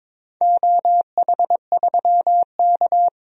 Morse code, heard as a keyed tone, about 21 wpm, 700 Hz. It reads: OH3K
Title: DXCC-100 Call Signs - 4 Characters - 22wpm